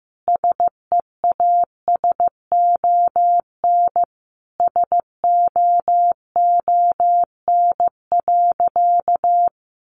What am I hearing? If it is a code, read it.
Morse code, heard as a keyed tone, about 15 wpm, 700 Hz. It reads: SEASON SOON.
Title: Sentences from Top 1400 Words 15wpm